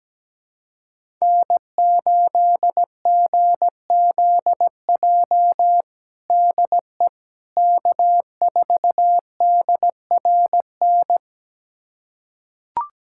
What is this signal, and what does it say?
Morse code, heard as a keyed tone, about 17 wpm, 700 Hz. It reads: N8GZJ DE K4DRN
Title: USA Callsign DE Callsign 17wpm